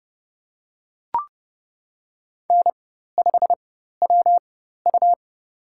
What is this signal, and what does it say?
Morse code, heard as a keyed tone, about 30 wpm, spaced as for 14 wpm, 700 Hz. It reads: N5WU
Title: USA Call Signs - 4 Characters - 4x Character Spacing - 30wpm